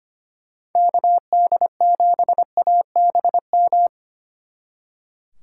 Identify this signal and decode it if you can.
Morse code, heard as a keyed tone, about 26 wpm, 700 Hz. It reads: KD7ABM
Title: Speed Racing - USA 2x3 Call Signs 25wpm